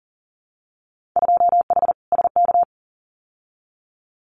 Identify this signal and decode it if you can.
Morse code, heard as a keyed tone, about 39 wpm, 700 Hz. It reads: 2H SK
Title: Winter Field Day Exchanges Encore 40wpm